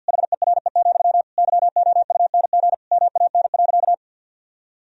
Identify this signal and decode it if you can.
Morse code, heard as a keyed tone, about 52 wpm, 700 Hz. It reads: HERE, YOUNG MAN.
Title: Sentences from Top 1000 Words - Review Part 1 - 50wpm